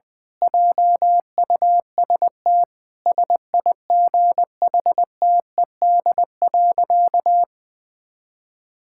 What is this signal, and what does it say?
Morse code, heard as a keyed tone, about 20 wpm, 700 Hz. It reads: JUST SIGHTED.
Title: Pangram Challenge v2 - 20wpm